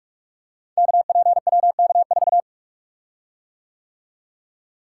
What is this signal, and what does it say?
Morse code, heard as a keyed tone, about 44 wpm, 700 Hz. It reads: KPWK4
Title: Code Groups - 5 Letter:Numbers - 45wpm